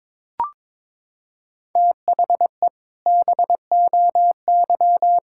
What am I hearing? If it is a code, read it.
Morse code, heard as a keyed tone, about 21 wpm, 700 Hz. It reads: THE BOY
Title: Sentences from Top 1600 Words 22wpm